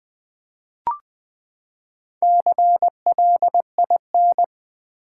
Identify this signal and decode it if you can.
Morse code, heard as a keyed tone, about 20 wpm, 700 Hz. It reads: CLIN
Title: Build A Word - 4 Letters At A Time - 8-Letter Words - 20wpm